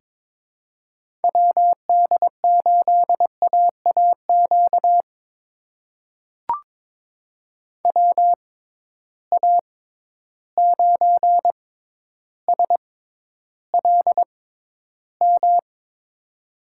Morse code, heard as a keyed tone, about 22 wpm, spaced as for 11 wpm, 700 Hz. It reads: WD8AAQ WA9SLM
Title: USA Call Signs - 6 Characters - 6x Character Spacing - 22wpm